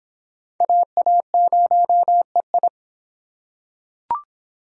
Morse code, heard as a keyed tone, about 24 wpm, 700 Hz. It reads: AA0EI